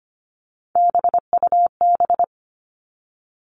Morse code, heard as a keyed tone, about 25 wpm, 700 Hz. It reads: BUB
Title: All Cut Numbers - 3-Digit 25wpm